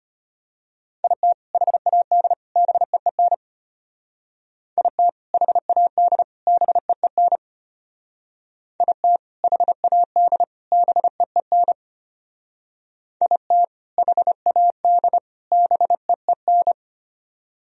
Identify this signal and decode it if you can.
Morse code, heard as a keyed tone, about 33 wpm, 700 Hz. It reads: IT HAD BEEN IT HAD BEEN IT HAD BEEN IT HAD BEEN